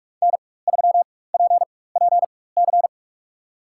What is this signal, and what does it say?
Morse code, heard as a keyed tone, about 42 wpm, spaced as for 21 wpm, 700 Hz. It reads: N3PPC
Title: USA Call Signs - 5 Characters - 4x Character Spacing - 45wpm